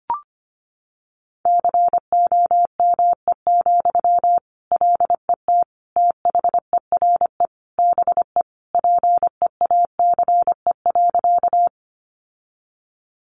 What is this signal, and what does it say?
Morse code, heard as a keyed tone, about 25 wpm, 700 Hz. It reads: COME, LET THERE BE PEACE.
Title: Sentences from Top 1500 Words 25wpm